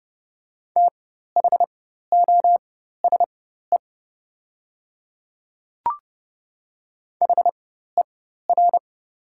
Morse code, heard as a keyed tone, about 29 wpm, spaced as for 14 wpm, 690 Hz. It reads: THOSE HER
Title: 4x Character Spacing - Top 500 Words - 30wpm